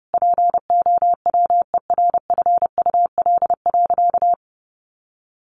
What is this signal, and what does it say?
Morse code, heard as a keyed tone, about 30 wpm, 700 Hz. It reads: POWERFUL.
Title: Sentences from Top 1300 Words 30wpm